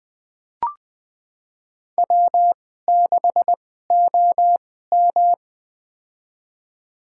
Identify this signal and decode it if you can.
Morse code, heard as a keyed tone, about 20 wpm, spaced as for 14 wpm, 700 Hz. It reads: W6OM
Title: USA Call Signs - 4 Characters - 2x Character Spacing - 20wpm